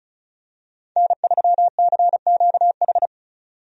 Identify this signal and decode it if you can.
Morse code, heard as a keyed tone, about 34 wpm, 700 Hz. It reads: N3CQH
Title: USA Call Signs - 5 Characters - 35wpm